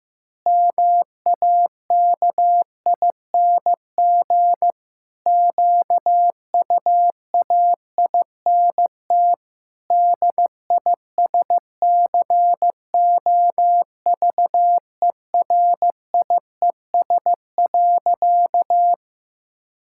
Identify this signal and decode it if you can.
Morse code, heard as a keyed tone, about 15 wpm, 700 Hz. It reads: MAKING QUAINT DISCOVERIES.
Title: Pangram Challenge v2 - 15wpm